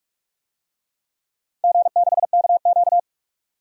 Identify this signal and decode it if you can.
Morse code, heard as a keyed tone, about 44 wpm, 700 Hz. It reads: G6KQ
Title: DXCC-100 Call Signs - 4 Characters - 45wpm